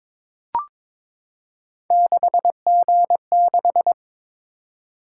Morse code, heard as a keyed tone, about 21 wpm, 700 Hz. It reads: 6G6
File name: practice / 22wpm / Three Letter-Number Combinations - 22wpm